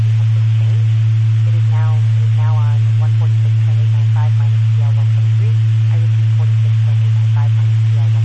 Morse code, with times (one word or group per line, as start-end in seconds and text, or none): none